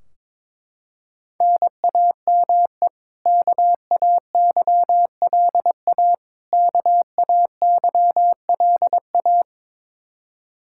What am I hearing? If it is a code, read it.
Morse code, heard as a keyed tone, about 22 wpm, 700 Hz. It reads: NAME KAYLA KAYLA
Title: Practice QSO - Exchange 1 - Beginning - 22wpm